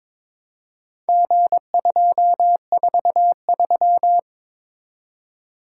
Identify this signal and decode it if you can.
Morse code, heard as a keyed tone, about 22 wpm, 700 Hz. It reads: G243
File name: practice / 22wpm / Four Letter-Number Combinations - 22wpm